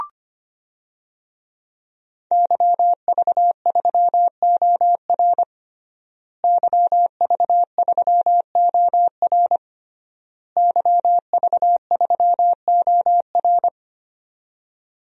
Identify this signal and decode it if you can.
Morse code, heard as a keyed tone, about 25 wpm, 700 Hz. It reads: YV3OR YV3OR YV3OR
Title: DXCC-100 Call Signs - 5 Characters - Repeated 3x - 25wpm